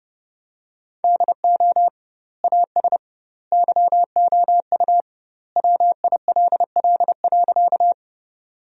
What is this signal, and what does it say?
Morse code, heard as a keyed tone, about 32 wpm, 700 Hz. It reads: DO AS YOU WILL.
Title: 2x Word-Spacing - Sentences from Top 100 Words - 30wpm